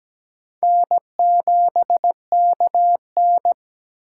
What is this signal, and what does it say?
Morse code, heard as a keyed tone, about 17 wpm, 700 Hz. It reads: N7KN
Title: USA Call Signs - 4 Characters - 8x Character Spacing - 17wpm